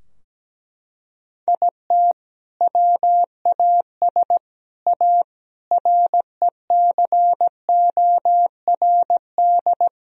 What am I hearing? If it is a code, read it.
Morse code, heard as a keyed tone, about 17 wpm, 700 Hz. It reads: IT WAS A RECORD
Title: Sentences from Top 1000 Words - Review Part 3 - 17wpm